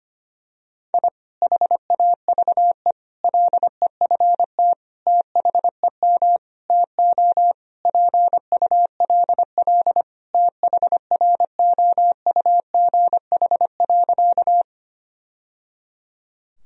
Morse code, heard as a keyed tone, about 25 wpm, 700 Hz. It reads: I HAVE LEFT THEM TO PULL THROUGH.
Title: Sentences from Top 1000 Words - Review Part 2 - 25wpm